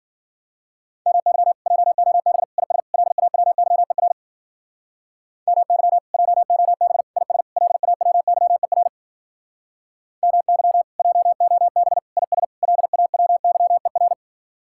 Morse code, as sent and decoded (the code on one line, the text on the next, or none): -- -.-- / .--- --- -... / .. ... / .-.. .- .-- -.-- . .-. / -- -.-- / .--- --- -... / .. ... / .-.. .- .-- -.-- . .-. / -- -.-- / .--- --- -... / .. ... / .-.. .- .-- -.-- . .-.
MY JOB IS LAWYER MY JOB IS LAWYER MY JOB IS LAWYER